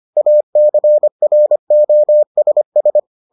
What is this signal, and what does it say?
Morse code, heard as a keyed tone, about 25 wpm, 600 Hz. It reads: ACROSS